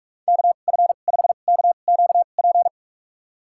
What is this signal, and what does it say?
Morse code, heard as a keyed tone, about 46 wpm, spaced as for 33 wpm, 700 Hz. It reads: KF5KQP
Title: USA Call Signs - 6 Characters - 2x Character Spacing - 45wpm